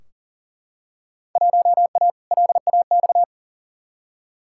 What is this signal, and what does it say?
Morse code, heard as a keyed tone, about 40 wpm, 700 Hz. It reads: 1A LAX
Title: Field Day Exchanges Encore - 40wpm